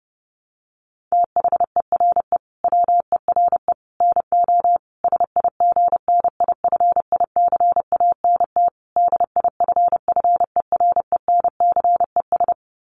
Morse code, heard as a keyed tone, about 30 wpm, 700 Hz. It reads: THERE WERE NO SIGNIFICANT DIFFERENCES
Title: Sets of 5 Words Encore2 30wpm